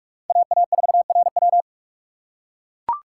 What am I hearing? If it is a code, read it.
Morse code, heard as a keyed tone, about 44 wpm, 700 Hz. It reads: AA4RW